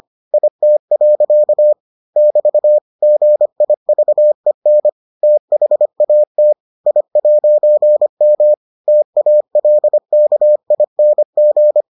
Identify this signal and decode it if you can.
Morse code, heard as a keyed tone, about 25 wpm, 600 Hz. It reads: IT. <BT> GIVEN THAT I'M TALKING